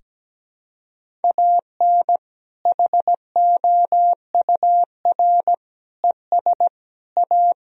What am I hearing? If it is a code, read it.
Morse code, heard as a keyed tone, about 17 wpm, 700 Hz. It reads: AN HOUR ES A
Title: Sentences from Top 400 Words and Abbreviations 17wpm